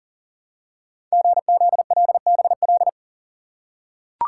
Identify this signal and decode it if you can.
Morse code, heard as a keyed tone, about 42 wpm, 700 Hz. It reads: GZLBL
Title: Code Groups - 5 Letters - 40wpm